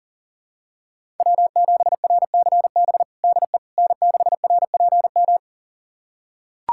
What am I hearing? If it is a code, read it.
Morse code, heard as a keyed tone, about 40 wpm, 700 Hz. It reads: W7RCB DE N6RPM
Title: USA Callsign DE Callsign - 5 Characters - 40wpm